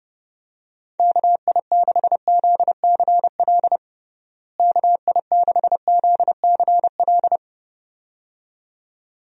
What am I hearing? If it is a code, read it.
Morse code, heard as a keyed tone, about 30 wpm, 700 Hz. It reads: KI6ZCL KI6ZCL
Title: USA Call Signs - 6 Characters - Repeated 2x - 30wpm